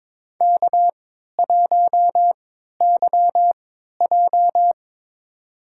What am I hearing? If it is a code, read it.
Morse code, heard as a keyed tone, about 22 wpm, spaced as for 12 wpm, 690 Hz. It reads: K1YJ
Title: USA Call Signs - 4 Characters - 3x Character Spacing - 22wpm